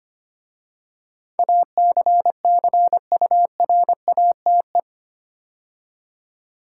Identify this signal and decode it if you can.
Morse code, heard as a keyed tone, about 25 wpm, 700 Hz. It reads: ACCURATE
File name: 8 Letter Words - First 3 Letters Given - 25wpm